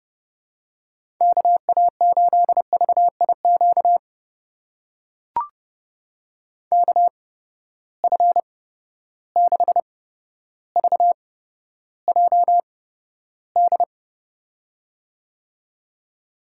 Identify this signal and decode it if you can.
Morse code, heard as a keyed tone, about 30 wpm, spaced as for 13 wpm, 700 Hz. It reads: KA8VIQ KF6VJD
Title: USA Call Signs - 6 Characters - 8x Character Spacing - 30wpm